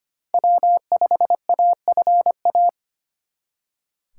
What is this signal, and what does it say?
Morse code, heard as a keyed tone, about 33 wpm, 700 Hz. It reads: W5AFA